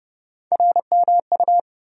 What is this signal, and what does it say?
Morse code, heard as a keyed tone, about 31 wpm, 700 Hz. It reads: RMU
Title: USA Call Sign Postfixes - 3 Letters 30wpm